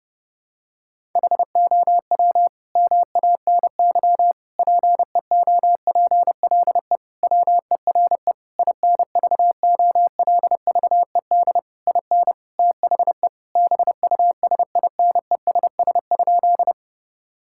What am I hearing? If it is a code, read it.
Morse code, heard as a keyed tone, about 32 wpm, 700 Hz. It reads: HOW MANY PEOPLE WERE INVOLVED IN THE BUSINESS?
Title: Sentences from Top 1000 Words - Review Part 5 - 30wpm